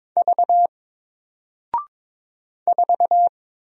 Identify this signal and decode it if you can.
Morse code, heard as a keyed tone, about 22 wpm, 700 Hz. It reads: V 4
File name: Single-Letter-Number-22wpm